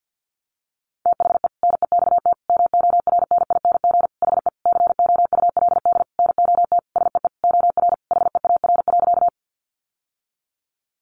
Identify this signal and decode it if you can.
Morse code, heard as a keyed tone, about 50 wpm, 700 Hz. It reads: THE NEXT MORNING HE COULD NOT SEE OR HEAR.
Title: Sentences from Top 600 Words 50wpm